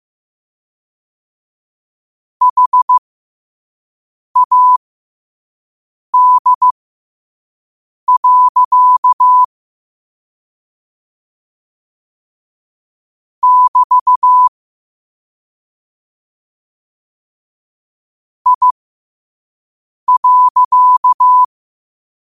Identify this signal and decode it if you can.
Morse code, heard as a keyed tone, about 15 wpm, spaced as for 5 wpm, 1000 Hz. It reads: HAD. <BT> I.
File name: qotdfiles / 5wpm